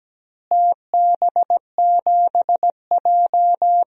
Morse code, heard as a keyed tone, about 17 wpm, 700 Hz. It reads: TB7J
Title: DXCC-100 Call Signs - 4 Characters - Repeated 3x - 17wpm